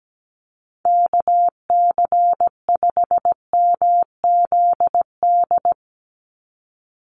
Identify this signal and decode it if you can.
Morse code, heard as a keyed tone, about 17 wpm, 700 Hz. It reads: KC5MZD
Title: USA Call Signs 17wpm